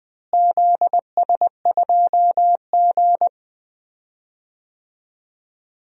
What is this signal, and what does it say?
Morse code, heard as a keyed tone, about 20 wpm, 700 Hz. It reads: ZS2G
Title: DXCC-100 Call Signs - 4 Characters - Repeated 2x - 20wpm